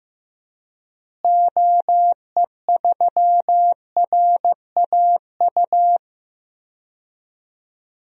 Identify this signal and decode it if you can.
Morse code, heard as a keyed tone, about 15 wpm, 700 Hz. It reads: OE3RAU
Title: DXCC-100 Call Signs - 6 Characters - Repeated 3x - 15wpm